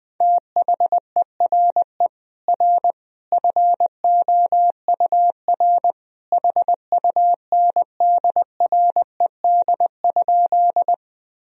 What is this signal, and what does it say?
Morse code, heard as a keyed tone, about 20 wpm, 700 Hz. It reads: THERE R FOUR HUNDRED?
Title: Sentences from Top 400 Words and Abbreviations 20wpm